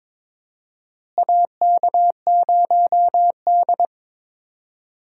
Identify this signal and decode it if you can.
Morse code, heard as a keyed tone, about 22 wpm, 700 Hz. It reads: AK0D
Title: USA Call Signs - 4 Characters - Repeated 3x - 22wpm